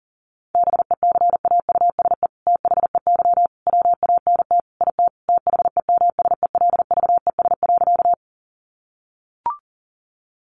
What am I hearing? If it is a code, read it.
Morse code, heard as a keyed tone, about 40 wpm, 700 Hz. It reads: BECAUSE THEY WANT IT THEMSELVES.
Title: Sentences from Top 800 Words 40wpm